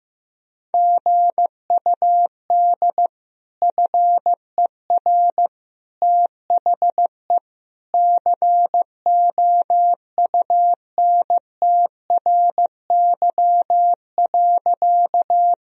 Morse code, heard as a keyed tone, about 15 wpm, 700 Hz. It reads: GUD FER THE COUNTRY.
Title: Sentences from Top 300 Words and Abbreviations 15wpm